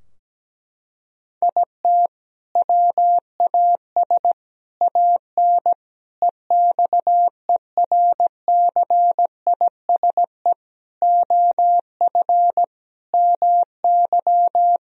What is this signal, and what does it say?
Morse code, heard as a keyed tone, about 17 wpm, 700 Hz. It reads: IT WAS AN EXERCISE OF MY
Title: Sentences from Top 1000 Words - Review Part 4 - 17wpm